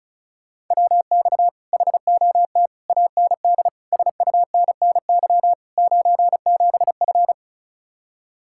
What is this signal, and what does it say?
Morse code, heard as a keyed tone, about 43 wpm, 700 Hz. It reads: WX HOT AND SUNNY 97F
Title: Speed-Racing Common QSO Phrases - 35wpm